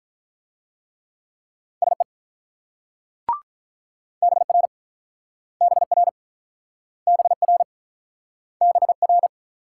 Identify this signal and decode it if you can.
Morse code, heard as a keyed tone, about 44 wpm, 700 Hz. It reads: SE BR BR BR BR